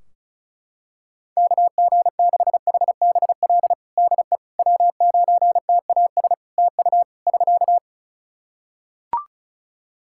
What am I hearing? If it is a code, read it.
Morse code, heard as a keyed tone, about 36 wpm, 700 Hz. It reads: KG6HBL DE W9TAS TU <SK>
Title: Practice QSO - Exchange 4 - Ending - 35wpm